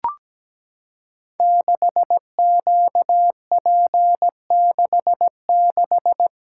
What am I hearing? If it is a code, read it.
Morse code, heard as a keyed tone, about 17 wpm, 700 Hz. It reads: 6QP66
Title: Code Groups - 5 Letter:Numbers - 17wpm